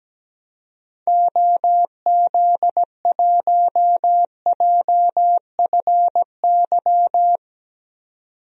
Morse code, heard as a keyed tone, about 17 wpm, 700 Hz. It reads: OZ1JFY
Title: DXCC-100 Call Signs - 6 Characters - 17wpm